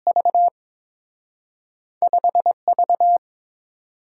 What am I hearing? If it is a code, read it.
Morse code, heard as a keyed tone, about 30 wpm, 700 Hz. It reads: V 5 V